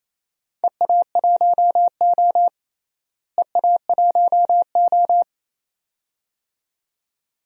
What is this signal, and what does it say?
Morse code, heard as a keyed tone, about 28 wpm, 700 Hz. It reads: EA1O EA1O
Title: DXCC-100 Call Signs - 4 Characters - Repeated 2x - 28wpm